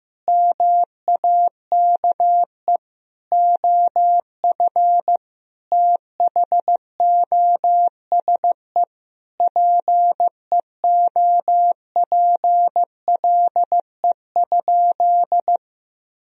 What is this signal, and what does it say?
Morse code, heard as a keyed tone, about 15 wpm, 700 Hz. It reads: MAKE OF THOSE PEOPLE?
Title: Sentences from Top 1000 Words - Review Part 1 - 15wpm